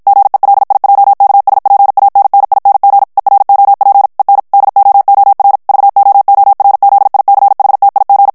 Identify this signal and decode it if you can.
Morse code, heard as a keyed tone, about 56 wpm, 800 Hz. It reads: GELIJKSPANNING EROP EN DOOR VOORZICHTIG
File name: HOLLANDAIS texte clair 2 vitesse 50 mots minute